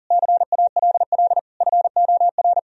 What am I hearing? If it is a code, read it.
Morse code, heard as a keyed tone, about 40 wpm, 700 Hz. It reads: CALL FOR
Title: Sentences from Top 1000 Words - Review Part 4 - 40wpm